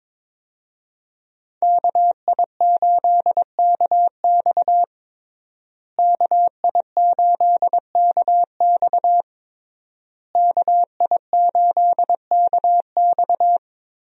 Morse code, heard as a keyed tone, about 21 wpm, 700 Hz. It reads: KI8KX KI8KX KI8KX
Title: USA Call Signs - 5 Characters - Repeated 3x - 22wpm